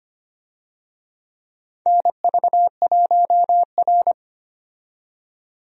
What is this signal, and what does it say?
Morse code, heard as a keyed tone, about 25 wpm, 700 Hz. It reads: NV1R
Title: USA Call Signs - 4 Characters - 25wpm